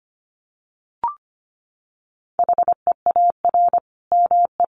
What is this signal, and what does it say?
Morse code, heard as a keyed tone, about 25 wpm, 700 Hz. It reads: HEAR ME
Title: Sentences from Top 1400 Words 25wpm